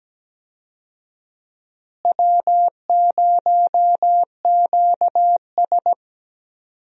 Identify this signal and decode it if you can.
Morse code, heard as a keyed tone, about 17 wpm, 700 Hz. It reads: W0QS
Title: USA Call Signs - 4 Characters - Repeated 3x - 17wpm